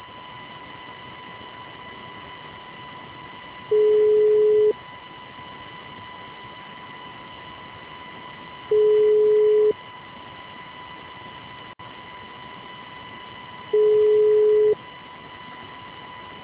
Ringback tone (Europe, 425 Hz).
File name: out- -2029- - - .39899